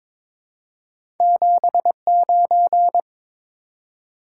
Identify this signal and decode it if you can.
Morse code, heard as a keyed tone, about 22 wpm, 700 Hz. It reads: 79
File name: practice / 22wpm / Two Letter-Number Combinations - 22wpm